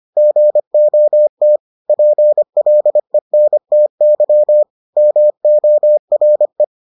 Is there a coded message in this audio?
Yes: Morse code from 0.2 to 6.6 s, heard as a keyed tone, about 25 wpm, 600 Hz. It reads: GOT PLENTY MORE